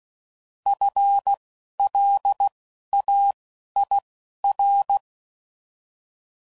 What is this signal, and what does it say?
Morse code, heard as a keyed tone, about 16 wpm, 800 Hz. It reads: F L A I R